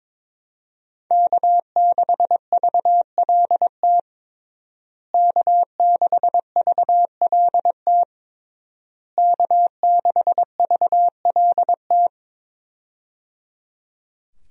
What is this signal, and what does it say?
Morse code, heard as a keyed tone, about 21 wpm, 700 Hz. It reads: K6VLT K6VLT K6VLT
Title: USA Call Signs - 5 Characters - Repeated 3x - 22wpm